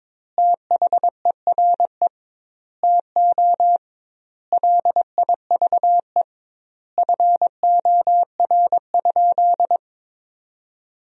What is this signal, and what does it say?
Morse code, heard as a keyed tone, about 22 wpm, 700 Hz. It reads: THERE TO LIVE FOR?
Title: 2x Word-Spacing - Sentences from Top 200 Words - 22wpm